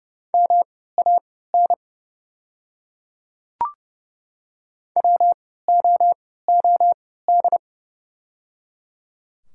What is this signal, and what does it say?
Morse code, heard as a keyed tone, about 30 wpm, spaced as for 17 wpm, 700 Hz. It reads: MAN WOOD
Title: 3x Character Spacing - Top 300 Words - 30wpm